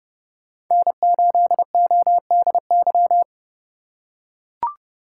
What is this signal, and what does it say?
Morse code, heard as a keyed tone, about 30 wpm, 700 Hz. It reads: N8ODY
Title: USA Call Signs - 5 Characters - 4x Character Spacing - 30wpm